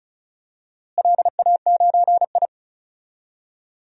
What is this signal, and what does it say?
Morse code, heard as a keyed tone, about 36 wpm, 700 Hz. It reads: LA9I
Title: DXCC-100 Call Signs - 4 Characters - Repeated 3x - 35wpm